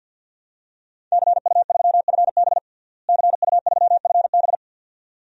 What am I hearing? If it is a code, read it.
Morse code, heard as a keyed tone, about 46 wpm, 700 Hz. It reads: CU3FB CU3FB
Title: DXCC-100 Call Signs - 5 Characters - Repeated 2x - 50wpm